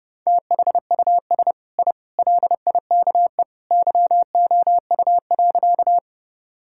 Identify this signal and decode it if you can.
Morse code, heard as a keyed tone, about 30 wpm, 700 Hz. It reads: THUS I LIKE YOU.
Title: Sentences from Top 1000 Words - Review Part 5 - 30wpm